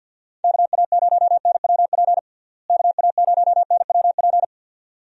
Morse code, heard as a keyed tone, about 46 wpm, 700 Hz. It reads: KA0NWP KA0NWP
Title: USA Call Signs - 6 Characters - Repeated 2x - 50wpm